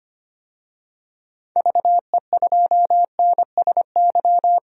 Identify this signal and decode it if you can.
Morse code, heard as a keyed tone, about 25 wpm, 700 Hz. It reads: VE2NSY
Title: DXCC-100 Call Signs - 6 Characters - 25wpm